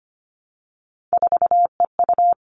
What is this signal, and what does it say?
Morse code, heard as a keyed tone, about 25 wpm, 700 Hz. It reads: 4EU
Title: All Cut Numbers - 3-Digit 25wpm